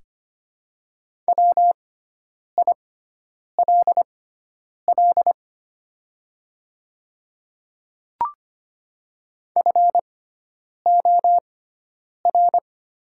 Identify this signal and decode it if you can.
Morse code, heard as a keyed tone, about 25 wpm, spaced as for 9 wpm, 700 Hz. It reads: WILL FOR
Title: 6x Character Spacing - Top 100 Words - 25wpm